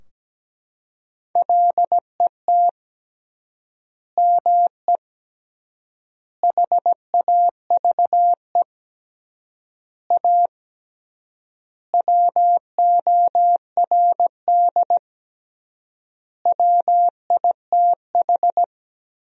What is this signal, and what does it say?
Morse code, heard as a keyed tone, about 17 wpm, 700 Hz. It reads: LET ME HAVE A WORD WITH
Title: 3x Word-Spacing - Sentences from Top 300 Words - 17wpm